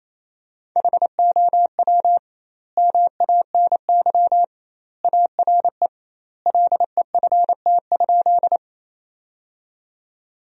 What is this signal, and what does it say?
Morse code, heard as a keyed tone, about 28 wpm, 700 Hz. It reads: HOW MANY ARE LEFT?
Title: 2x Word-Spacing - Sentences from Top 300 Words - 28wpm